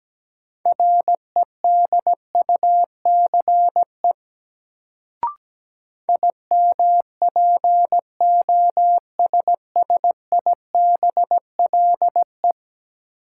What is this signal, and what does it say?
Morse code, heard as a keyed tone, about 17 wpm, 700 Hz. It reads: REDUCE IMPOSSIBLE
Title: Sentences from Top 1100 Words - New Word Drill Alt - 17wpm